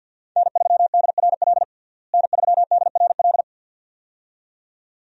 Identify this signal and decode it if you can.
Morse code, heard as a keyed tone, about 45 wpm, 700 Hz. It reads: N3DRL N3DRL
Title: USA Call Signs - 5 Characters - Repeated 2x - 50wpm